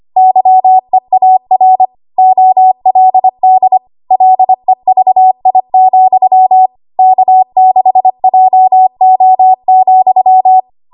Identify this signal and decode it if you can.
Morse code, heard as a keyed tone, about 25 wpm, 750 Hz. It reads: YEAR OLD LEVI, K6JO,
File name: Archive / 25 WPM / 25WPM